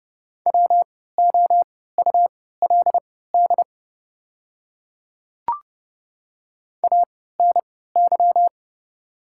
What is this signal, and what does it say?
Morse code, heard as a keyed tone, about 30 wpm, spaced as for 17 wpm, 690 Hz. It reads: WOULD ANY
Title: 3x Character Spacing - Top 200 Words - 30wpm